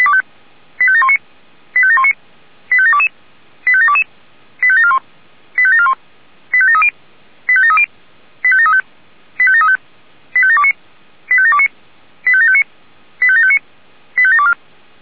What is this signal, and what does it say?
Five-tone selective call (ZVEI), heard as quick runs of five tones, tones of about 70 ms. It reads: 85719 85719 85722 85722 85741 85741 85730 85730 85735 85735 85729 85729 85759 85759 85724
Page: SDR/ZVEI